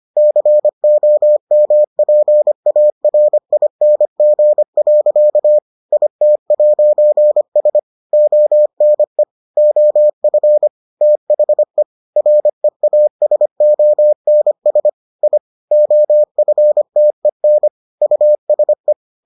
Morse code, heard as a keyed tone, about 25 wpm, 600 Hz. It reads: COMPARING. IT'S ONE OF THE REASONS I OFTEN USE